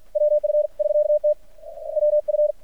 Morse code, heard as a keyed tone, about 50 wpm, 600 Hz. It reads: WF <SK>T WF
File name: archived / CRSnet